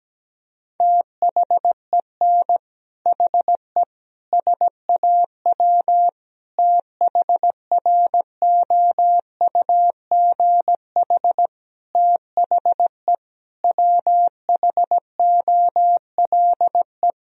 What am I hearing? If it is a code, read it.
Morse code, heard as a keyed tone, about 17 wpm, 700 Hz. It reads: THEN HE SAW THROUGH THE WHOLE
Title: Sentences from Top 1000 Words - Review Part 1 - 17wpm